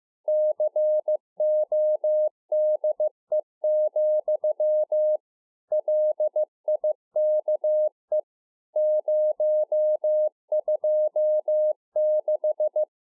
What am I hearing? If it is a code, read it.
Morse code, heard as a keyed tone, about 15 wpm, 610 Hz. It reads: CODE, LIKE 026
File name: 2011-04-mp3 / A SMALL TRANCEIVER 2 15WPM